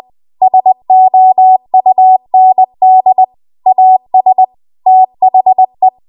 Morse code, heard as a keyed tone, about 20 wpm, 750 Hz. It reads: SOUND AS THE
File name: Archive / 20 WPM / 20WPM